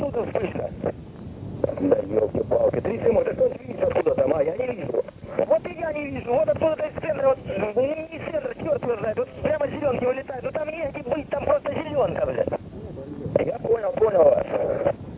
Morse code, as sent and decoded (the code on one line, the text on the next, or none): none